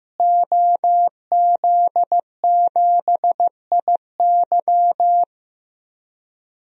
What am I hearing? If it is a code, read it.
Morse code, heard as a keyed tone, about 15 wpm, 700 Hz. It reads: OZ7IY
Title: DXCC-100 Call Signs - 5 Characters - Repeated 3x - 15wpm